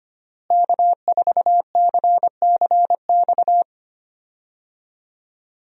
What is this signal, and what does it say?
Morse code, heard as a keyed tone, about 25 wpm, 700 Hz. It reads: K4CCX
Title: USA Call Signs - 5 Characters - Repeated 3x - 25wpm